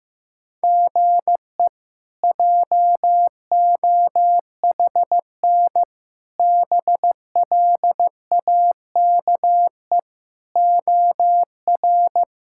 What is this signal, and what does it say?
Morse code, heard as a keyed tone, about 16 wpm, 700 Hz. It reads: GE JOHN BLAKE OR
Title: K1USN SST QSO Practice - 15wpm